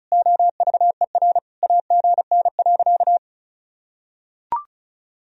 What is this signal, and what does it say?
Morse code, heard as a keyed tone, about 36 wpm, 700 Hz. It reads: OVER AGN.
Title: Sentences from Top 500 Words and Abbreviations 35wpm